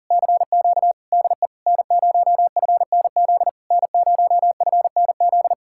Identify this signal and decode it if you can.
Morse code, heard as a keyed tone, about 48 wpm, 700 Hz. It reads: CQ DE N0FNZ N0FNZ
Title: Speed-Racing Common QSO Phrases - 40wpm